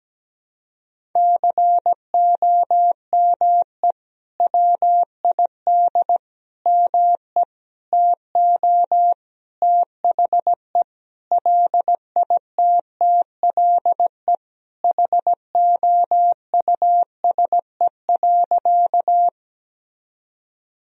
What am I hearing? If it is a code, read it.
Morse code, heard as a keyed tone, about 17 wpm, 700 Hz. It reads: COME WID ME TO THE LITTLE HOUSE.
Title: Sentences from Top 200 Words and Abbreviations 17wpm